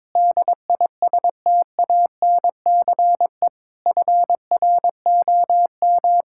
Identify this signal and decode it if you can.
Morse code, heard as a keyed tone, about 21 wpm, 700 Hz. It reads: DISTANCE FROM
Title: Sentences from Top 1600 Words 22wpm